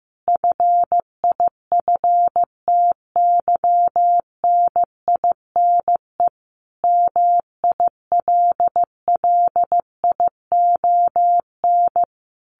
Morse code, heard as a keyed tone, about 15 wpm, 700 Hz. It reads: FIFTYNINE MILLION
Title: A Sound Of Thunder 15wpm